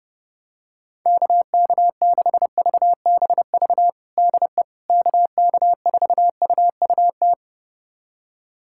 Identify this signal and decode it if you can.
Morse code, heard as a keyed tone, about 30 wpm, 700 Hz. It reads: KK6VBV DE KK4UUT
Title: USA Callsign DE Callsign - 6 Characters - 30wpm